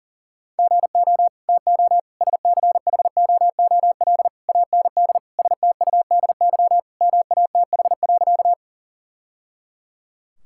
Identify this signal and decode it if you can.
Morse code, heard as a keyed tone, about 40 wpm, 700 Hz. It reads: GO TO SCHOOL AND STUDY MATH.
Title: Sentences from Top 1000 Words - Review Part 6 - 40wpm